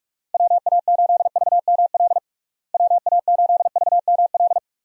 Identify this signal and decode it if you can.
Morse code, heard as a keyed tone, about 46 wpm, 700 Hz. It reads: WA8VML WA8VML
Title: USA Call Signs - 6 Characters - Repeated 3x - 45wpm